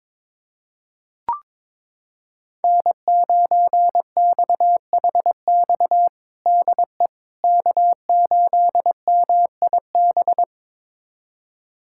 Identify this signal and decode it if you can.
Morse code, heard as a keyed tone, about 21 wpm, 700 Hz. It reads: N9XHX DE K8MIB
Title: USA Callsign DE Callsign - 5 Characters - 22wpm